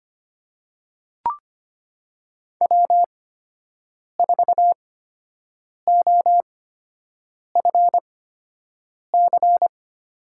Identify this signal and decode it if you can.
Morse code, heard as a keyed tone, about 25 wpm, spaced as for 7 wpm, 700 Hz. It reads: W4OFC
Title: USA Call Signs - 5 Characters - 8x Character Spacing - 25wpm